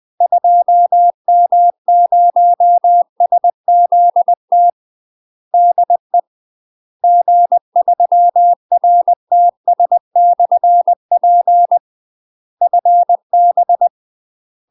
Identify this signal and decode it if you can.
Morse code, heard as a keyed tone, about 20 wpm, 700 Hz. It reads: 2M0SZT DE G3RTS/P FB
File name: UK MASTER QSOs TRAINING / 20WPM / UK MASTER QSOs 0005